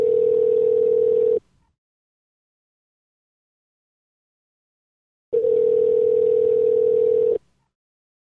Ringback tone (North America), 440 + 480 Hz.